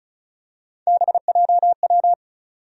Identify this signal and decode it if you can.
Morse code, heard as a keyed tone, about 34 wpm, 700 Hz. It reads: BJW
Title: USA Call Sign Postfixes - 3 Letters 35wpm